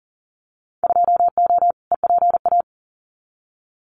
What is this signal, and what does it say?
Morse code, heard as a keyed tone, about 40 wpm, 700 Hz. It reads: 2O EPA
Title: Winter Field Day Exchanges Encore 40wpm